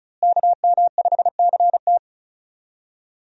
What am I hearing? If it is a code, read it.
Morse code, heard as a keyed tone, about 35 wpm, 700 Hz. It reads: KM5CT
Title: USA Call Signs - 5 Characters - Repeated 2x - 35wpm